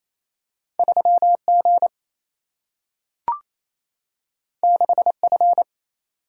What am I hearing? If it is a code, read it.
Morse code, heard as a keyed tone, about 28 wpm, 700 Hz. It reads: 3G 6F
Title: Single-Letter Single-Number 28wpm